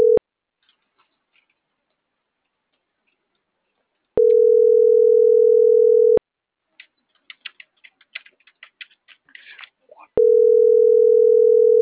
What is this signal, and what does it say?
Ringback tone (North America), 440 + 480 Hz.